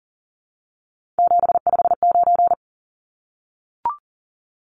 Morse code, heard as a keyed tone, about 40 wpm, 700 Hz. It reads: 759
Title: Numbers - 3-Digit 40wpm